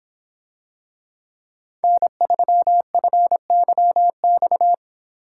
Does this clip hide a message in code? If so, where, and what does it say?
Yes: Morse code from 1.8 to 4.7 s, heard as a keyed tone, about 26 wpm, 700 Hz. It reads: N3FYX